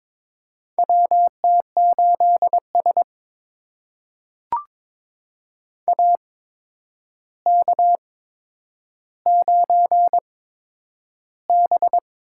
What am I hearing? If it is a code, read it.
Morse code, heard as a keyed tone, about 22 wpm, spaced as for 9 wpm, 700 Hz. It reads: WT8S AK9B
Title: USA Call Signs - 4 Characters - 8x Character Spacing - 22wpm